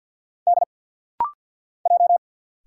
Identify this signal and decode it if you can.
Morse code, heard as a keyed tone, about 50 wpm, 700 Hz. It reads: D J
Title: Single Letters - Rapid-Fire 50wpm